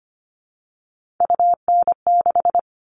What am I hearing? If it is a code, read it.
Morse code, heard as a keyed tone, about 25 wpm, 700 Hz. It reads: UN6
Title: All Cut Numbers - 3-Digit 25wpm